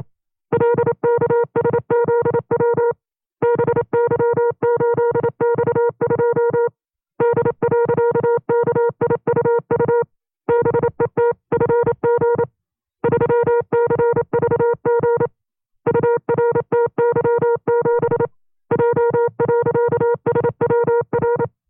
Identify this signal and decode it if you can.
Morse code, heard as a keyed tone, about 27 wpm, 470 Hz. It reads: LKSZW BY8X2 D.KIUU BETFG 3CVG URTY7 J.SWR
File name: 25WPM